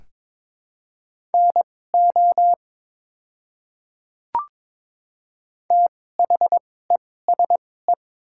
Morse code, heard as a keyed tone, about 22 wpm, spaced as for 16 wpm, 700 Hz. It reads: NO THESE
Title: 2x Character Spacing - Top 100 Words - 22wpm